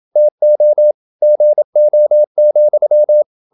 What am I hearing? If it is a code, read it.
Morse code, heard as a keyed tone, about 27 wpm, 600 Hz. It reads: TO GO,
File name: Never 0027wpm